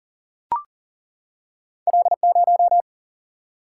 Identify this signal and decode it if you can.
Morse code, heard as a keyed tone, about 42 wpm, 700 Hz. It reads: L0
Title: Single-Letter Single-Number 40wpm